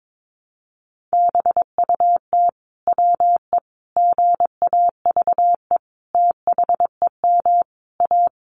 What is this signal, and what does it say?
Morse code, heard as a keyed tone, about 22 wpm, 700 Hz. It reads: BUT WE GAVE THEM A
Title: Sentences from Top 1200 Words 22wpm